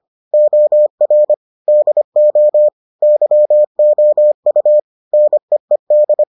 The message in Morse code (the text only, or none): OR DO YOU NEED